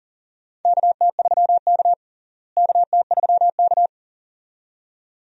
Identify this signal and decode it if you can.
Morse code, heard as a keyed tone, about 39 wpm, 700 Hz. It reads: KT3K KT3K
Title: USA Call Signs - 4 Characters - Repeated 2x - 40wpm